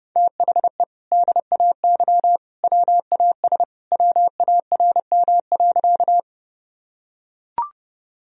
Morse code, heard as a keyed tone, about 30 wpm, 700 Hz. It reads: THE DAY WAS WARM.
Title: Sentences from Top 1000 Words - Review Part 2 - 30wpm